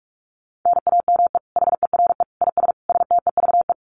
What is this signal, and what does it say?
Morse code, heard as a keyed tone, about 44 wpm, 700 Hz. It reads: NAME HERE IS STEVE
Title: Common QSO Phrases 45wpm